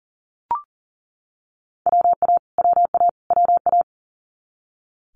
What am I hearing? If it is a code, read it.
Morse code, heard as a keyed tone, about 40 wpm, 690 Hz. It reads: WA WA WA
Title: Multi-letter Phonemes Repeated 3x 40wpm